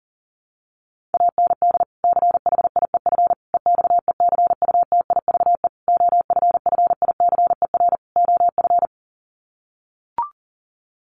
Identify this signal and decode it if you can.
Morse code, heard as a keyed tone, about 40 wpm, 700 Hz. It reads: AND CHIEF EXECUTIVE OFFICER OF
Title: Sets of 5 Words Encore2 40wpm